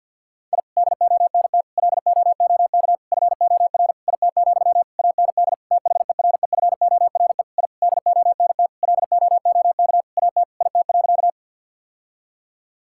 Morse code, heard as a keyed tone, about 47 wpm, 700 Hz. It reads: I DONT LOOK FOR IT, AND THEREFORE I DONT LOOK AT IT.
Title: Sentences from Top 1000 Words - Review Part 5 - 50wpm